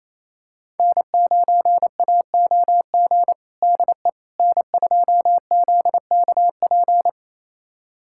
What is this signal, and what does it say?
Morse code, heard as a keyed tone, about 28 wpm, 700 Hz. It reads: N9AOG DE N2ZKP
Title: USA Callsign DE Callsign - 5 Characters - 28wpm